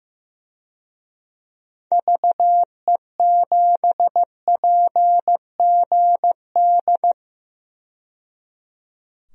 Morse code, heard as a keyed tone, about 15 wpm, 700 Hz. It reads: VE7PGD
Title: DXCC-100 Call Signs - 6 Characters - 15wpm